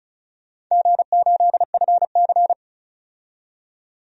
Morse code, heard as a keyed tone, about 36 wpm, 700 Hz. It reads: G8FC
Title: DXCC-100 Call Signs - 4 Characters - 35wpm